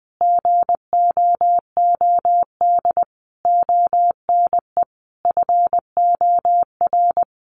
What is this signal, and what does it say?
Morse code, heard as a keyed tone, about 20 wpm, 700 Hz. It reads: GOOD ONE FOR
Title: Sentences from Top 1500 Words 20wpm